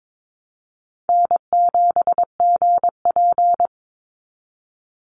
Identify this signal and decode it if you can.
Morse code, heard as a keyed tone, about 22 wpm, 700 Hz. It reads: N7GP
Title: USA Call Signs 22wpm